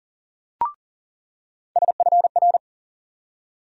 Morse code, heard as a keyed tone, about 40 wpm, 700 Hz. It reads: SFR
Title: USA Call Sign Postfixes - 3 Letters 40wpm